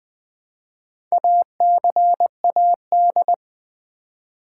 Morse code, heard as a keyed tone, about 20 wpm, 700 Hz. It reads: ACAD
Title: Build A Word - 2 Letters At A Time - 8-Letter Words - 20wpm